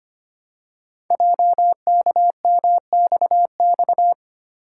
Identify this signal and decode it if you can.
Morse code, heard as a keyed tone, about 25 wpm, 700 Hz. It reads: JKMXX
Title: Code Groups - 5 Letters - 25wpm